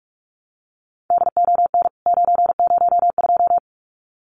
Morse code, heard as a keyed tone, about 44 wpm, 700 Hz. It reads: DON 902
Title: CWT Exchange 45wpm